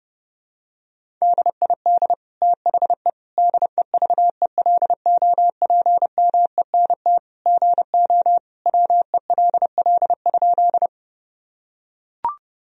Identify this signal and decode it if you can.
Morse code, heard as a keyed tone, about 30 wpm, 700 Hz. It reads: DID THE DEVELOPMENT GO WELL?
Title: Sentences from Top 1000 Words - Review Part 4 - 30wpm